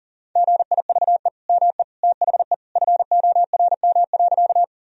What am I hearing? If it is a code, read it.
Morse code, heard as a keyed tone, about 40 wpm, 700 Hz. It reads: GIVE ME THE FORM.
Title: Sentences from Top 1000 Words - Review Part 1 - 40wpm